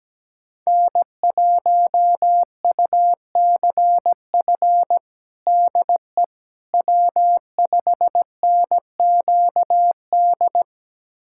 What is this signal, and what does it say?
Morse code, heard as a keyed tone, about 17 wpm, 700 Hz. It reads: N1UCF DE W5NQD
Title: USA Callsign DE Callsign - 5 Characters - 17wpm